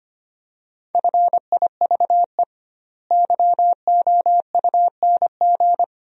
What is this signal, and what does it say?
Morse code, heard as a keyed tone, about 25 wpm, 700 Hz. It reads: FIVE YOUNG
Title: 2x Word-Spacing - Sentences from Top 400 Words - 25wpm